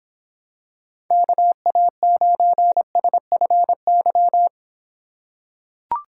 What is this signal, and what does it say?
Morse code, heard as a keyed tone, about 26 wpm, 700 Hz. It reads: KA9SFY